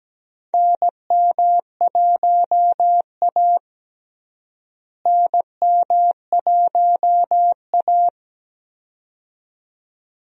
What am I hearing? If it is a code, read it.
Morse code, heard as a keyed tone, about 17 wpm, 700 Hz. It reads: NM1A NM1A
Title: USA Call Signs - 4 Characters - Repeated 2x - 17wpm